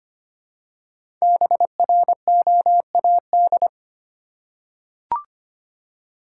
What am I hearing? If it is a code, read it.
Morse code, heard as a keyed tone, about 25 wpm, 700 Hz. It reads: BROAD
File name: 5 Letter Words - First 3 Letters Given - 25wpm